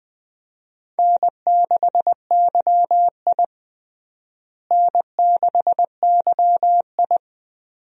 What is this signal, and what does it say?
Morse code, heard as a keyed tone, about 20 wpm, 700 Hz. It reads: N6YI N6YI
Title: USA Call Signs - 4 Characters - Repeated 2x - 20wpm